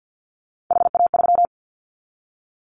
Morse code, heard as a keyed tone, about 49 wpm, 700 Hz. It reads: HA3
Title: Callsign Prefixes for Top 25 DXCC - Repeated 3x - 50wpm